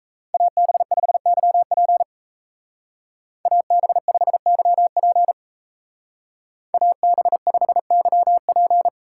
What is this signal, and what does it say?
Morse code, heard as a keyed tone, about 36 wpm, 700 Hz. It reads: AB5YP AB5YP AB5YP